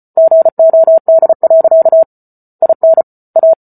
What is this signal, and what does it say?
Morse code, heard as a keyed tone, about 34 wpm, 650 Hz. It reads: GOD. IN A